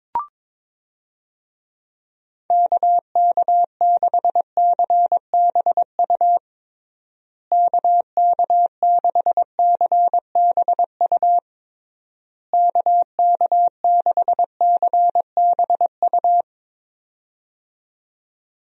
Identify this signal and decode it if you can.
Morse code, heard as a keyed tone, about 21 wpm, 700 Hz. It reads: KK6CBU KK6CBU KK6CBU
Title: USA Call Signs - 6 Characters - Repeated 3x - 22wpm